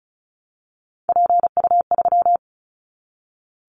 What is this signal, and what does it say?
Morse code, heard as a keyed tone, about 36 wpm, 700 Hz. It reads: PU3
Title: Callsign Prefixes for Top 100 DXCC - 35wpm